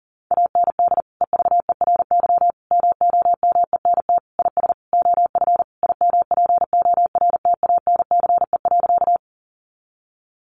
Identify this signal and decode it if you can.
Morse code, heard as a keyed tone, about 40 wpm, 700 Hz. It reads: AND EVERY MOMENT IS OF IMPORTANCE.
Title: Sentences from Top 1200 Words 40wpm